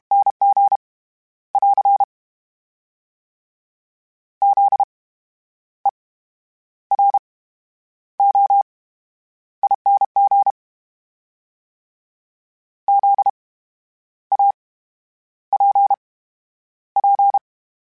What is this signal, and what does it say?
Morse code, heard as a keyed tone, about 32 wpm, spaced as for 10 wpm, 800 Hz. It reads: NG <AR> ZEROING ZAPP
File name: Special INGs / Special INGs-mp3s / VerySlowToFaster Ing @30